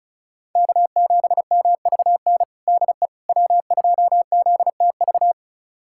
Morse code, heard as a keyed tone, about 36 wpm, 700 Hz. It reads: K7MVN DE W2ZTV
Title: USA Callsign DE Callsign 35wpm